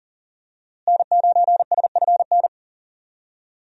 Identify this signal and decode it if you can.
Morse code, heard as a keyed tone, about 40 wpm, 700 Hz. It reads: N9SFN
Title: USA Call Signs - 5 Characters - Repeated 3x - 40wpm